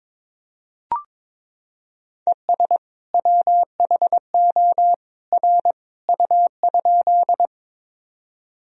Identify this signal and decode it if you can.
Morse code, heard as a keyed tone, about 22 wpm, 700 Hz. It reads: ES WHO R U?
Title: Sentences from Top 100 Words and Abbreviations 22wpm